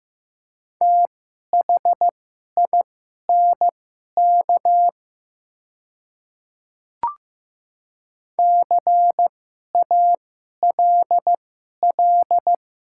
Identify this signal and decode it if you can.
Morse code, heard as a keyed tone, about 15 wpm, spaced as for 11 wpm, 700 Hz. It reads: THINK CALL
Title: 2x Character Spacing - Top 200 Words - 15wpm